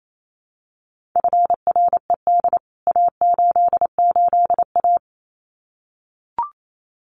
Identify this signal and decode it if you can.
Morse code, heard as a keyed tone, about 28 wpm, 700 Hz. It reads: FRED A88A
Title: CWT Exchange 28wpm